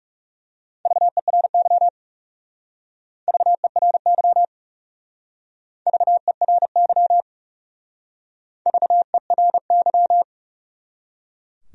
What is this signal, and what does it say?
Morse code, heard as a keyed tone, about 36 wpm, 700 Hz. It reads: VERY VERY VERY VERY
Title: Speed-Racing Top 300 Words 30wpm